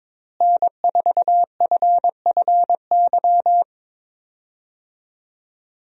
Morse code, heard as a keyed tone, about 22 wpm, 700 Hz. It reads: N4FFY
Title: USA Call Signs - 5 Characters - Repeated 3x - 22wpm